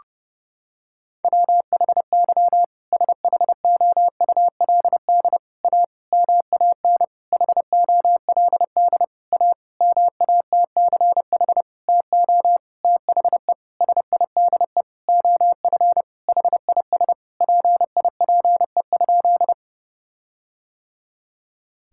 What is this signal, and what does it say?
Morse code, heard as a keyed tone, about 32 wpm, 700 Hz. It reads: WHY SHOULD A MAN HOLD A MATCH TO THE SIDE OF HIS PIPE?
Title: Sentences from Top 1600 Words 30wpm